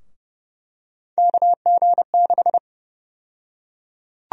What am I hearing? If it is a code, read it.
Morse code, heard as a keyed tone, about 30 wpm, 700 Hz. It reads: KG6
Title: USA Call Sign Prefixes - 30wpm